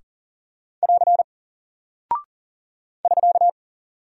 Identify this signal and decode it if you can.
Morse code, heard as a keyed tone, about 40 wpm, 700 Hz. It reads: <AR> <SK>
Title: Single Character (Letter:Number:Punctuation:Prosign) Alt 40wpm